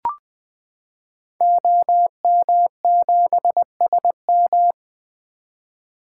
Morse code, heard as a keyed tone, about 20 wpm, 700 Hz. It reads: OM7SM
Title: 2023 SOTA Activator Call Signs - 20wpm